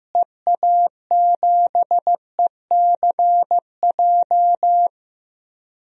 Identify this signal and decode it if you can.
Morse code, heard as a keyed tone, about 15 wpm, 700 Hz. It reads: EA7ECJ
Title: DXCC-100 Call Signs - 6 Characters - Repeated 3x - 15wpm